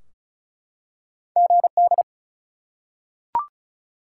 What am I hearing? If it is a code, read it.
Morse code, heard as a keyed tone, about 36 wpm, 700 Hz. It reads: GD
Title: Top 180 QSO Elements 35wpm